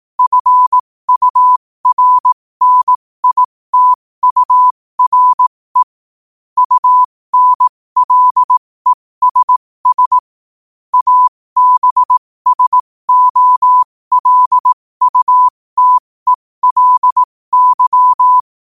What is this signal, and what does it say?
Morse code, heard as a keyed tone, about 18 wpm, spaced as for 15 wpm, 1000 Hz. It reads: FURNITURE UNLESS ABSOLUTELY
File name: qotdfiles / 15wpm